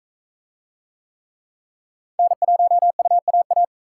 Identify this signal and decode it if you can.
Morse code, heard as a keyed tone, about 38 wpm, 700 Hz. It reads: N1UAA